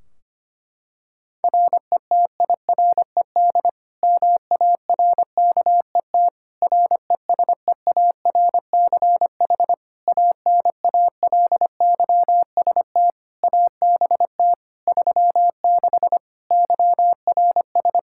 Morse code, heard as a keyed tone, about 25 wpm, 700 Hz. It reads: RETIRED MARKET RESEARCH ANALYST ABT 36 YRS
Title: Practice QSO - Exchange 3 - More Info - 25wpm